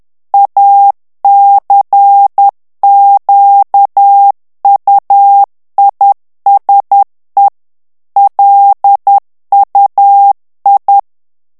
Morse code, heard as a keyed tone, about 10 wpm, 800 Hz. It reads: ACQUISE LUI
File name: CW / FICHIERS-MP3 / FRANCAIS TEXTE CLAIR 2 VITESSE 10 MOTS MINUTES